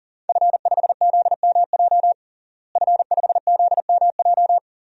Morse code, heard as a keyed tone, about 40 wpm, 700 Hz. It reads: F5ZMJ F5ZMJ
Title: DXCC-100 Call Signs - 5 Characters - Repeated 2x - 40wpm